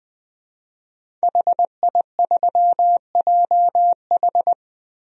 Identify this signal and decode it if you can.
Morse code, heard as a keyed tone, about 21 wpm, 700 Hz. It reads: HI3JH
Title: DXCC-100 Call Signs - 5 Characters - Repeated 2x - 20wpm